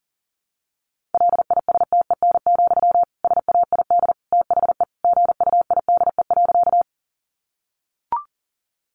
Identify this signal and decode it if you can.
Morse code, heard as a keyed tone, about 40 wpm, 700 Hz. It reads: LISTEN, SAID THE GUIDE.
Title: Sentences from Top 600 Words 40wpm